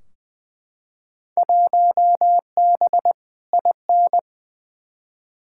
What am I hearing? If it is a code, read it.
Morse code, heard as a keyed tone, about 21 wpm, 700 Hz. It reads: 1B IN
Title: Field Day Exchanges - 20wpm